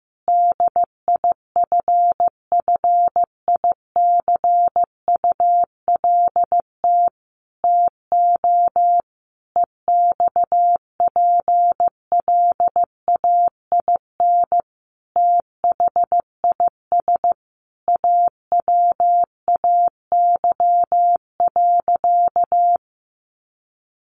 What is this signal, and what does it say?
Morse code, heard as a keyed tone, about 15 wpm, 700 Hz. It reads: DIFFICULT TO EXPLAIN THIS AWAY.
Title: Sentences from Top 1100 Words 15wpm